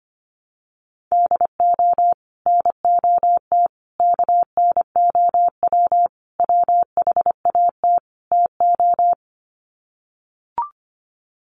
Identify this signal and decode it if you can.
Morse code, heard as a keyed tone, about 25 wpm, 700 Hz. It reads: DO NOT KNOW WHAT TO
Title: Sets of 5 Words 25wpm